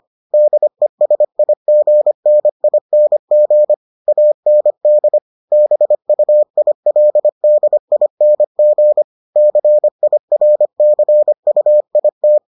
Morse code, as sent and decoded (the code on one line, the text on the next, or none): -.. ... .. --. -. .. -. --. / .- -. -.. / -... ..- .. .-.. -.. .. -. --. / -.-. .. .-. -.-. ..- .. -
DESIGNING AND BUILDING CIRCUIT